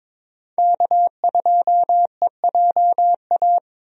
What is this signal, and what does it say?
Morse code, heard as a keyed tone, about 21 wpm, 700 Hz. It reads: K2EJA
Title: USA Call Signs - 5 Characters - 22wpm